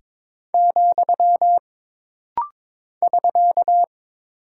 Morse code, heard as a keyed tone, about 22 wpm, 700 Hz. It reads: , <SK>
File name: practice / 22wpm / Single Punctuation-Prosign - Rapid-Fire - 22wpm